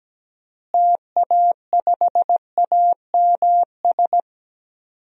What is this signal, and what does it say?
Morse code, heard as a keyed tone, about 17 wpm, 700 Hz. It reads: TA5AMS
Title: DXCC-100 Call Signs - 6 Characters - Repeated 3x - 17wpm